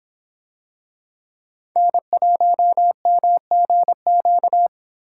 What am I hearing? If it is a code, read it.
Morse code, heard as a keyed tone, about 26 wpm, 700 Hz. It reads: N1MGQ